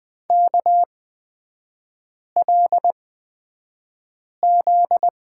Morse code, heard as a keyed tone, about 20 wpm, spaced as for 5 wpm, 700 Hz. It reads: KLZ
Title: 243 Lesson 39e - Letter z - Three Characters 20wpm